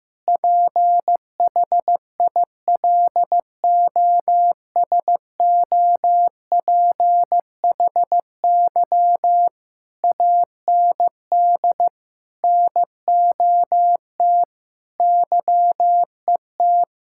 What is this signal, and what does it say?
Morse code, heard as a keyed tone, about 15 wpm, 700 Hz. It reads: PHILOSOPHY AND NOT YET
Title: Sentences from Top 1000 Words - Review Part 6 - 15wpm